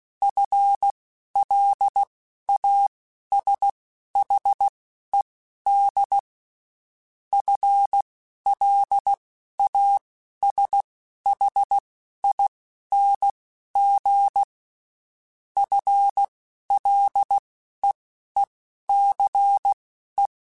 Morse code, heard as a keyed tone, about 16 wpm, spaced as for 11 wpm, 800 Hz. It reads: FLASHED FLASHING FLEECE
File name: Spaced / FL Runs sp mp3s / FL Runs sp @15